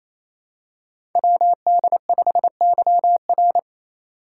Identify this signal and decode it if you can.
Morse code, heard as a keyed tone, about 27 wpm, 700 Hz. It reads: WD5YR
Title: Code Groups - 5 Letter:Numbers - 28wpm